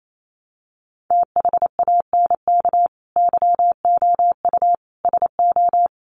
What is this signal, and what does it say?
Morse code, heard as a keyed tone, about 28 wpm, 700 Hz. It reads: THANK YOU SO
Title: Sets of 4 Words 28wpm